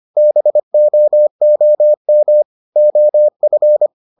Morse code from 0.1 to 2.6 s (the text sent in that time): BOOM